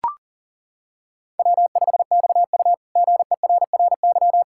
Morse code, heard as a keyed tone, about 40 wpm, 700 Hz. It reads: W5XU GERRY
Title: K1USN SST QSO Practice - 40wpm